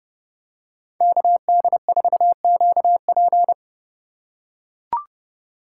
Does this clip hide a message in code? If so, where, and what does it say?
Yes: Morse code from 1.0 to 3.5 s, heard as a keyed tone, about 29 wpm, 700 Hz. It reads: KD4QP